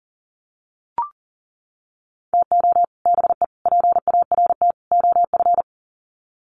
Morse code, heard as a keyed tone, about 40 wpm, 700 Hz. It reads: TO BE PART OF
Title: Sets of 4 Words 40wpm